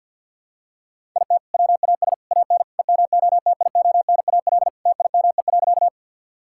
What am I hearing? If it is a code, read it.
Morse code, heard as a keyed tone, about 48 wpm, 700 Hz. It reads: IT WAS AN EMOTIONAL TIME.
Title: Sentences from Top 1000 Words - Review Part 5 - 50wpm